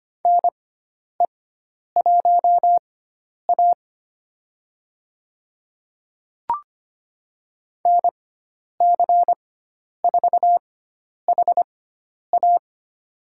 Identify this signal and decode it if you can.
Morse code, heard as a keyed tone, about 25 wpm, spaced as for 10 wpm, 700 Hz. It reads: NE1A NC4HA
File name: Lesson 16g - Letter c - Call Signs 25wpm